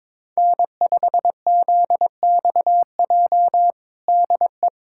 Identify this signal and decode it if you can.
Morse code, heard as a keyed tone, about 22 wpm, 700 Hz. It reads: N5ZXJ DE
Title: USA Callsign DE Callsign - 5 Characters - 22wpm